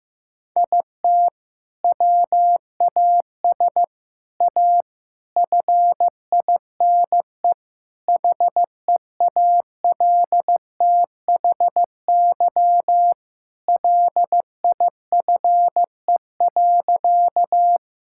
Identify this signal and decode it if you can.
Morse code, heard as a keyed tone, about 15 wpm, 700 Hz. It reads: IT WAS A FINE HEALTHY LIFE.
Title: Sentences from Top 1000 Words - Review Part 5 - 15wpm